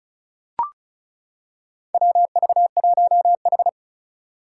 Morse code, heard as a keyed tone, about 36 wpm, 690 Hz. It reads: WV1H
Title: Four Challenging Letter-Number Combinations 35wpm